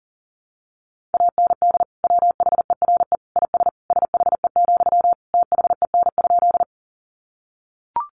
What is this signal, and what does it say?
Morse code, heard as a keyed tone, about 40 wpm, 700 Hz. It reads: AND WHERE IS SHE, THEN?
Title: Sentences from Top 200 Words 40wpm